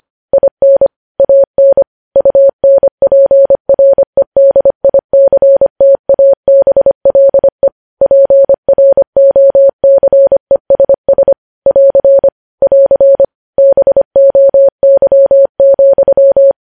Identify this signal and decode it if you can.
Morse code, heard as a keyed tone, about 25 wpm, 550 Hz. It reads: IN AN UNPREDICTABLE PROCESS <AR> <AR> BOY,
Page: news headlines in morse code at 25 wpm